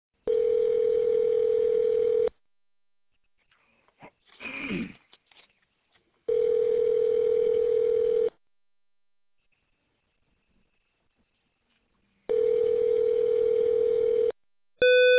Ringback tone (North America), 440 + 480 Hz.